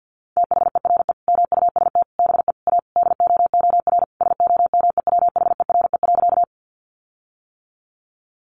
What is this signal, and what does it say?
Morse code, heard as a keyed tone, about 50 wpm, 700 Hz. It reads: THERE MUST BE A DOOR SOMEWHERE.
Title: Sentences from Top 1400 Words 50wpm